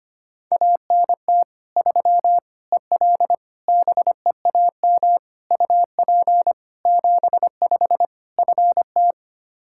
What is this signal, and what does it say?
Morse code, heard as a keyed tone, about 25 wpm, 700 Hz. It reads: ANT 3 EL BEAM UP 75 FT
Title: Practice QSO - Exchange 2 - Info - 25wpm